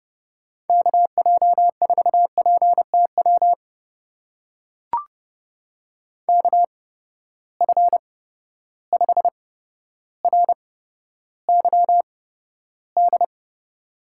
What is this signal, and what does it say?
Morse code, heard as a keyed tone, about 31 wpm, spaced as for 13 wpm, 700 Hz. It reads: KJ4PTW KF5RYD
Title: USA Call Signs - 6 Characters - 8x Character Spacing - 30wpm